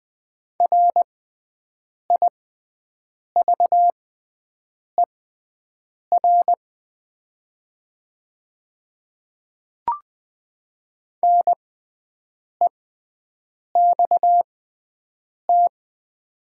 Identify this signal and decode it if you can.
Morse code, heard as a keyed tone, about 20 wpm, spaced as for 7 wpm, 700 Hz. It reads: RIVER NEXT
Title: 6x Character Spacing - Top 300 Words - 20wpm